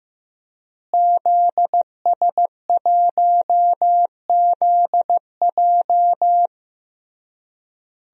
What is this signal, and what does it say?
Morse code, heard as a keyed tone, about 15 wpm, 700 Hz. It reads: ZS1ZJ
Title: DXCC-100 Call Signs - 5 Characters - Repeated 3x - 15wpm